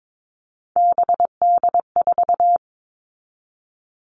Morse code, heard as a keyed tone, about 22 wpm, 700 Hz. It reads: BD4
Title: Callsign Prefixes for Top 100 DXCC - 22wpm